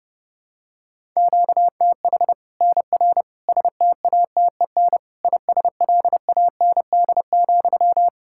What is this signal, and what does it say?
Morse code, heard as a keyed tone, about 32 wpm, 700 Hz. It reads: QTH NR STATEN ISLAND,
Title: Practice QSO - Exchange 1 - Beginning - 30wpm